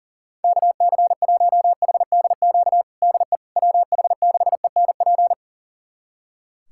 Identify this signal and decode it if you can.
Morse code, heard as a keyed tone, about 37 wpm, 700 Hz. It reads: KC1HDQ DE WH6ENP
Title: USA Callsign DE Callsign - 6 Characters - 40wpm